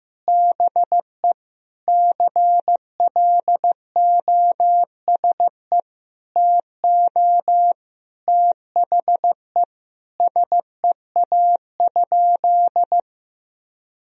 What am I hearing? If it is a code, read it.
Morse code, heard as a keyed tone, about 15 wpm, 700 Hz. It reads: BE CLOSE TO THE SEA?
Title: Sentences from Top 300 Words and Abbreviations 15wpm